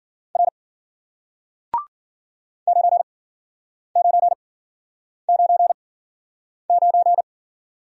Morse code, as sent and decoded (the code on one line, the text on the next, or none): .-. / ----. ----. ----. ----.
R 9999